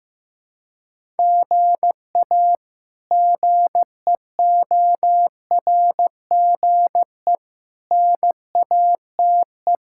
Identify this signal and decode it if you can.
Morse code, heard as a keyed tone, about 16 wpm, 700 Hz. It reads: GA GEORGE NATE
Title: K1USN SST QSO Practice - 15wpm